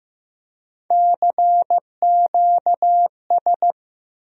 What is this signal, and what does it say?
Morse code, heard as a keyed tone, about 15 wpm, 700 Hz. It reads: CQS
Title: USA Call Sign Postfixes - 3 Letters 15wpm